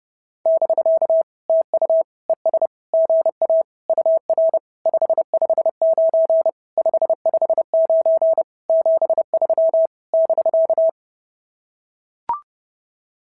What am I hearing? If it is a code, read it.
Morse code, heard as a keyed tone, about 29 wpm, 690 Hz. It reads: <BK> TU ES GA UR 559 559 73 <BK>
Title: SOTA QSO Practice 30wpm